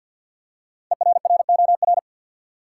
Morse code, heard as a keyed tone, about 52 wpm, 700 Hz. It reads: ERROR
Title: Five Letter Words - 50wpm